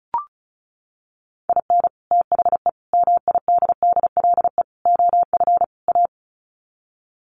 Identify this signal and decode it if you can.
Morse code, heard as a keyed tone, about 34 wpm, 700 Hz. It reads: IN THE MIDDLE OF A
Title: Sets of 5 Words 35wpm